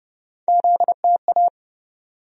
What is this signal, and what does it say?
Morse code, heard as a keyed tone, about 30 wpm, 700 Hz. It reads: ZTA
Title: USA Call Sign Postfixes - 3 Letters 30wpm